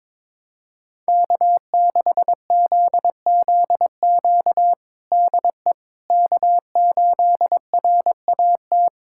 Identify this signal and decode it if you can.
Morse code, heard as a keyed tone, about 22 wpm, 700 Hz. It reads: K6ZZQ DE K8RAT
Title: USA Callsign DE Callsign 22wpm